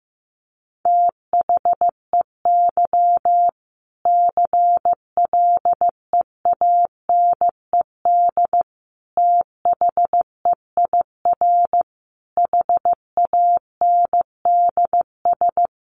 Morse code, heard as a keyed tone, about 15 wpm, 700 Hz. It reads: THEY CLEANED THEIR HANDS
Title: A Sound Of Thunder 15wpm